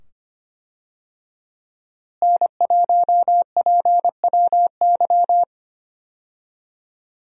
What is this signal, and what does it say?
Morse code, heard as a keyed tone, about 25 wpm, 700 Hz. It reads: N1PWY
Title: USA Call Signs - 5 Characters - Repeated 2x - 25wpm